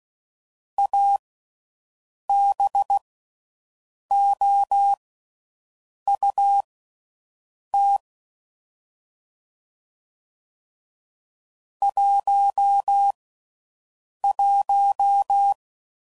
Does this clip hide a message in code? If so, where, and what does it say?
Yes: Morse code from 0.8 to 15.5 s, heard as a keyed tone, about 16 wpm, spaced as for 6 wpm, 800 Hz. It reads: ABOUT 11